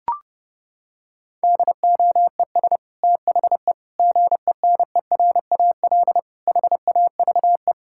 Morse code, heard as a keyed tone, about 30 wpm, 700 Hz. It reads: DOES THE GENERAL HAVE
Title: Sentences from Top 1000 Words - Review Part 6 - 30wpm